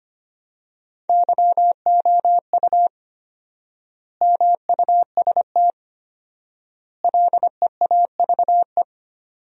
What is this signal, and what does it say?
Morse code, heard as a keyed tone, about 25 wpm, 700 Hz. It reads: YOU MUST LEAVE
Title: 4x Word-Spacing - Sentences from Top 400 Words - 25wpm